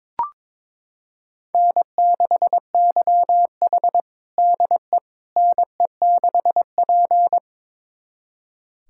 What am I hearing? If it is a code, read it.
Morse code, heard as a keyed tone, about 22 wpm, 700 Hz. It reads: N6YH DE NE6P
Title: USA Callsign DE Callsign - 4 Characters - 22wpm